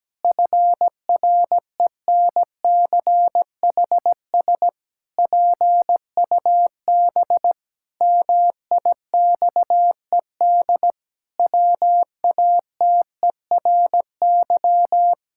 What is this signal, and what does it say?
Morse code, heard as a keyed tone, about 17 wpm, 700 Hz. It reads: FRENCHS PUB MIXED WATERY
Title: Pangram Challenge v2 - 17wpm